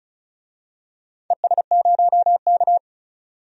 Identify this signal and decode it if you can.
Morse code, heard as a keyed tone, about 36 wpm, 690 Hz. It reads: ES0K
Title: DXCC-100 Call Signs - 4 Characters - 35wpm